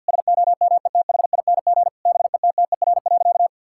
Morse code, heard as a keyed tone, about 51 wpm, 700 Hz. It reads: SOMETHING BETTER.
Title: Sentences from Top 1000 Words - Review Part 6 - 50wpm